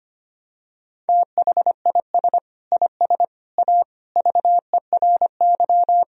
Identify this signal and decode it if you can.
Morse code, heard as a keyed tone, about 25 wpm, 700 Hz. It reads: THIS IS A VERY
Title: Sentences from Top 1000 Words - Review Part 1 - 25wpm